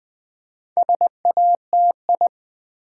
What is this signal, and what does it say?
Morse code, heard as a keyed tone, about 20 wpm, 700 Hz. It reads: SATI
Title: Build A Word - 4 Letters At A Time - 12-Letter Words - 20wpm